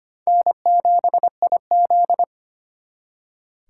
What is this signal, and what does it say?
Morse code, heard as a keyed tone, about 34 wpm, 700 Hz. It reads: N7IZ